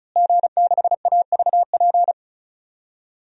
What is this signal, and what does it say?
Morse code, heard as a keyed tone, about 34 wpm, 700 Hz. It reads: G6AVP
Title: DXCC-100 Call Signs - 5 Characters - Repeated 3x - 35wpm